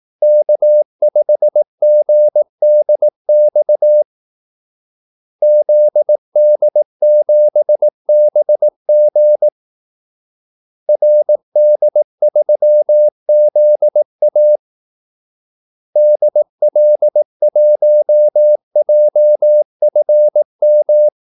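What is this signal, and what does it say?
Morse code, heard as a keyed tone, about 18 wpm, 600 Hz. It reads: K5GDX ZD7BG RD3ZA DL1JFM
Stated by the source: Morse call-sign practice at 18 wpm.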